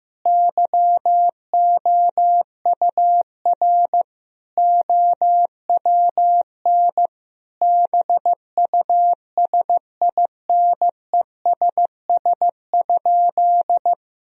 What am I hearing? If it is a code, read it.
Morse code, heard as a keyed tone, about 15 wpm, 700 Hz. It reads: YOUR OWN BUSINESS?
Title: Sentences from Top 1000 Words - Review Part 4 - 15wpm